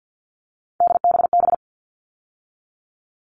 Morse code, heard as a keyed tone, about 48 wpm, 700 Hz. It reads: DBB
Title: All Cut Numbers - 3-Digit 50wpm